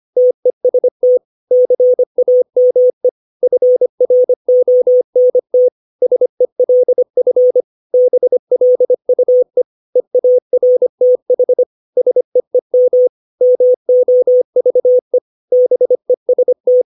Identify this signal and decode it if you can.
Morse code, heard as a keyed tone, about 25 wpm, 500 Hz. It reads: TEST CAME FRONT SELF BLUE EARTH SEEM MOVE BEST
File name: 632 words / 632 Words 25WPM